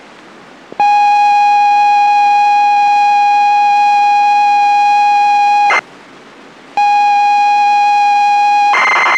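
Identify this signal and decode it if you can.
ZX Spectrum tape loading.
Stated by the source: ZX Spectrum tape loading from the game Deep Zone, dumped from the original cassette.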